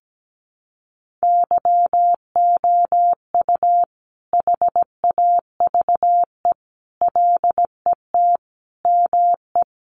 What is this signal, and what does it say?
Morse code, heard as a keyed tone, about 17 wpm, 700 Hz. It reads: YOU HAVE LET ME
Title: Sentences from Top 1500 Words 17wpm